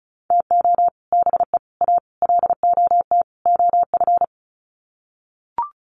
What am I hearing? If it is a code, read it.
Morse code, heard as a keyed tone, about 34 wpm, 700 Hz. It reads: TO BE A LOT OF
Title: Sets of 5 Words Encore 35wpm